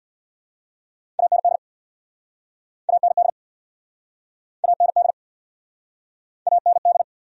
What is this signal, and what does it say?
Morse code, heard as a keyed tone, about 59 wpm, 700 Hz. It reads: AND AND AND AND
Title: Speed-Racing Top 200 Words 50wpm